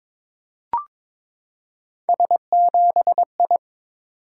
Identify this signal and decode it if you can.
Morse code, heard as a keyed tone, about 22 wpm, 700 Hz. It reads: S7I
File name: practice / 22wpm / Three Letter-Number Combinations - 22wpm